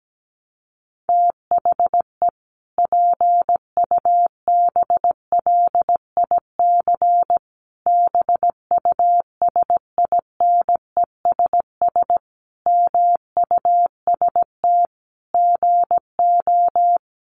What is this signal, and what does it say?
Morse code, heard as a keyed tone, about 17 wpm, 700 Hz. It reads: THE PUBLIC BUSINESS MUST GO
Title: Sentences from Top 1300 Words 17wpm